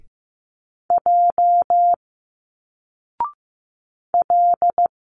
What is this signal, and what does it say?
Morse code, heard as a keyed tone, about 15 wpm, 700 Hz. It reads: J L
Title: Single Letters 15wpm